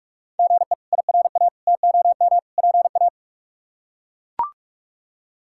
Morse code, heard as a keyed tone, about 42 wpm, 700 Hz. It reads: GE IRA TOM PA
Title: K1USN SST QSO Practice - 45wpm